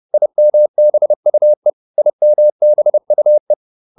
Morse code, heard as a keyed tone, about 30 wpm, 600 Hz. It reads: IMBUE IMBUE